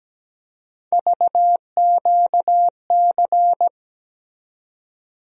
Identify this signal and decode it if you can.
Morse code, heard as a keyed tone, about 17 wpm, 700 Hz. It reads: VQC
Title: USA Call Sign Postfixes - 3 Letters 17wpm